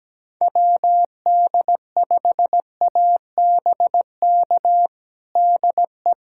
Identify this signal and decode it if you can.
Morse code, heard as a keyed tone, about 17 wpm, 700 Hz. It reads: WD5ABK DE
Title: USA Callsign DE Callsign 17wpm